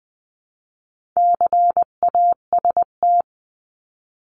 Morse code, heard as a keyed tone, about 20 wpm, 700 Hz. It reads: CAST
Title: 259 Lesson 42d - Words 18-20wpm